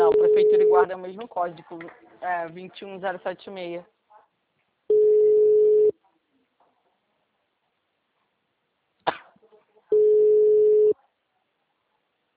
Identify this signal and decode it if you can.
Ringback tone (Europe, 425 Hz), with speech.
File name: out- -2033- - - .73638